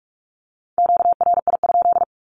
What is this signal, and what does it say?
Morse code, heard as a keyed tone, about 46 wpm, 700 Hz. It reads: QRI?
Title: Q-Codes with Spoken Meaning - 45wpm